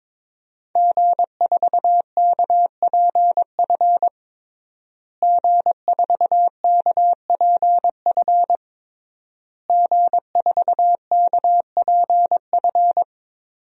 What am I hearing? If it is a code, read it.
Morse code, heard as a keyed tone, about 22 wpm, 700 Hz. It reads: G4KPF G4KPF G4KPF
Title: DXCC-100 Call Signs - 5 Characters - Repeated 3x - 22wpm